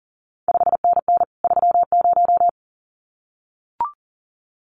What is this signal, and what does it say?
Morse code, heard as a keyed tone, about 40 wpm, 700 Hz. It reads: 5NN 30
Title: CQ WW DX Contest Exchange Encore - 40wpm